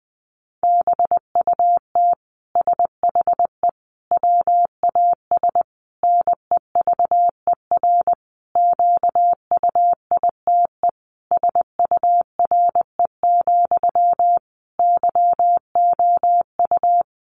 Morse code, heard as a keyed tone, about 19 wpm, 700 Hz. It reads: BUT SHE WAS NEVER QUITE SURE, YOU
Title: Sentences from Top 700 Words 20wpm